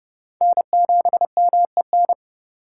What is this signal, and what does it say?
Morse code, heard as a keyed tone, about 30 wpm, 700 Hz. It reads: N7MEN
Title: USA Call Signs - 5 Characters - 4x Character Spacing - 30wpm